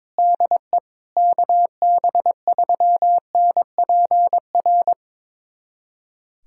Morse code, heard as a keyed tone, about 22 wpm, 700 Hz. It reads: DE KB3NPR
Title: USA Callsign DE Callsign - 6 Characters - 22wpm